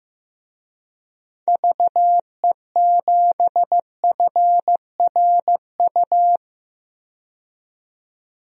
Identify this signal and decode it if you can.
Morse code, heard as a keyed tone, about 15 wpm, 700 Hz. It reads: VE7FRU
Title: DXCC-100 Call Signs - 6 Characters - 15wpm